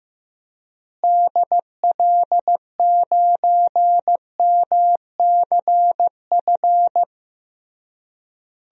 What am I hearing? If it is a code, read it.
Morse code, heard as a keyed tone, about 15 wpm, 700 Hz. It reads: DL9MCF
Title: DXCC-100 Call Signs - 6 Characters - Repeated 3x - 15wpm